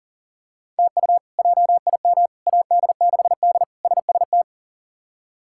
Morse code, heard as a keyed tone, about 40 wpm, 700 Hz. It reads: TU JIM AD6D SST
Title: K1USN SST QSO Practice - 40wpm